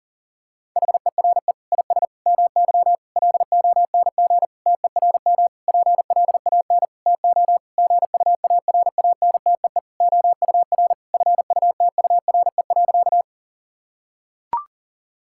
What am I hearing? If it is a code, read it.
Morse code, heard as a keyed tone, about 40 wpm, 700 Hz. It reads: HERE IS MY LONG TERM PLAN TO GUARANTEE OUR FUTURE.
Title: Sentences from Top 1600 Words 40wpm